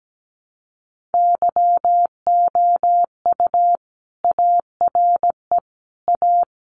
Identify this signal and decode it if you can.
Morse code, heard as a keyed tone, about 17 wpm, 700 Hz. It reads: YOU ARE A
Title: Sentences from Top 200 Words 17wpm